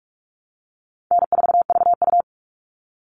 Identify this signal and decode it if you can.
Morse code, heard as a keyed tone, about 44 wpm, 700 Hz. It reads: N4VU
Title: All Cut Numbers - 4-Digit 45wpm